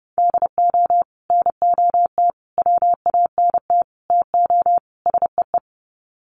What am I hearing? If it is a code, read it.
Morse code, heard as a keyed tone, about 30 wpm, 700 Hz. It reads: DO NOT WANT TO SEE
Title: Sets of 5 Words 30wpm